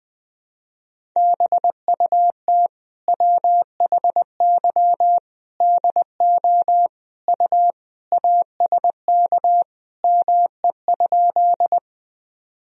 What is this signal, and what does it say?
Morse code, heard as a keyed tone, about 20 wpm, 700 Hz. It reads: BUT WHY DO U ASK ME?
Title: Sentences from Top 200 Words and Abbreviations 20wpm